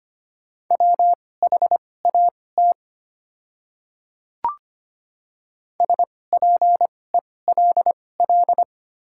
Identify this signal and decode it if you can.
Morse code, heard as a keyed tone, about 25 wpm, spaced as for 18 wpm, 690 Hz. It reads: WHAT SPELL
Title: 2x Character Spacing - Top 200 Words - 25wpm